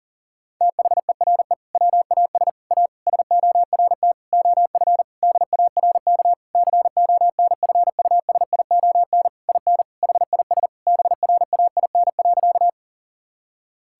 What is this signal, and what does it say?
Morse code, heard as a keyed tone, about 37 wpm, 700 Hz. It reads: THERE WAS A SORT OF DARK CONFUSION IN HIS BRAIN.
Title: Sentences from Top 1600 Words 40wpm